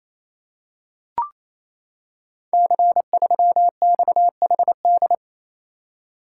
Morse code, heard as a keyed tone, about 28 wpm, 700 Hz. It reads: C3XHD
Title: Code Groups - 5 Letter:Numbers - 28wpm